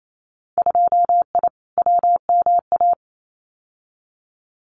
Morse code, heard as a keyed tone, about 28 wpm, 700 Hz. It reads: 2I WMA
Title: Winter Field Day Exchanges Encore 28wpm